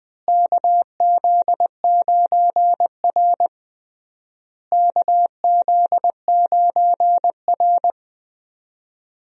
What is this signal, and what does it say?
Morse code, heard as a keyed tone, about 20 wpm, 700 Hz. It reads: KZ9R KZ9R
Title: USA Call Signs - 4 Characters - Repeated 3x - 20wpm